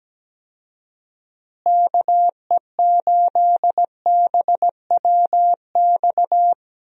Morse code, heard as a keyed tone, about 17 wpm, 700 Hz. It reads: KE8BWX
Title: USA Call Signs - 6 Characters - Repeated 3x - 17wpm